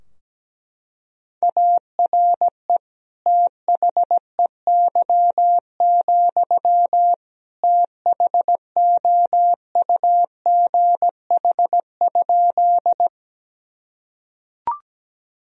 Morse code, heard as a keyed tone, about 17 wpm, 700 Hz. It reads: ARE THEY, THOUGH?
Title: Sentences from Top 1000 Words - Review Part 2 - 17wpm